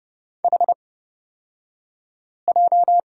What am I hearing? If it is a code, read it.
Morse code, heard as a keyed tone, about 31 wpm, spaced as for 5 wpm, 700 Hz. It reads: HJ
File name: Lesson 36d - BT - Two Characters 30wpm